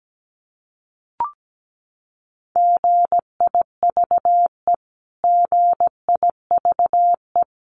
Morse code, heard as a keyed tone, about 17 wpm, 700 Hz. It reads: GIVE GIVE
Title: Top 200 Words Repeated 17wpm